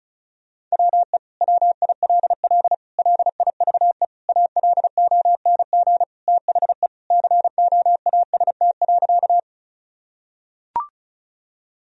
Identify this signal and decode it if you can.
Morse code, heard as a keyed tone, about 36 wpm, 700 Hz. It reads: WE WILL LIVE ALONG THE COAST.
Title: Sentences from Top 1000 Words - Review Part 4 - 35wpm